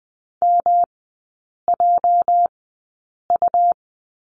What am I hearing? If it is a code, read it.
Morse code, heard as a keyed tone, about 20 wpm, spaced as for 8 wpm, 690 Hz. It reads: MJU
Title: 212 Lesson 34e - Number 8 - Three Characters 20wpm